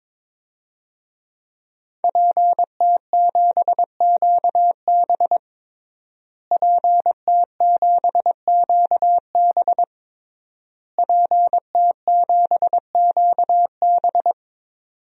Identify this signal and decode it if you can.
Morse code, heard as a keyed tone, about 22 wpm, 700 Hz. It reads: PT7QB PT7QB PT7QB
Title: DXCC-100 Call Signs - 5 Characters - Repeated 3x - 22wpm